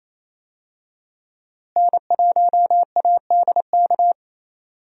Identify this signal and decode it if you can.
Morse code, heard as a keyed tone, about 28 wpm, 700 Hz. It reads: N1ADK
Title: USA Call Signs - 5 Characters - Repeated 3x - 28wpm